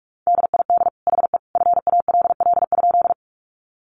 Morse code, heard as a keyed tone, about 44 wpm, 700 Hz. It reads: DID HE FALL?
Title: Sentences from Top 500 Words 45wpm